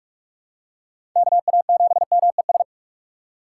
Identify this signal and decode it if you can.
Morse code, heard as a keyed tone, about 46 wpm, 700 Hz. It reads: KA7MES
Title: USA Call Signs - 6 Characters - 2x Character Spacing - 45wpm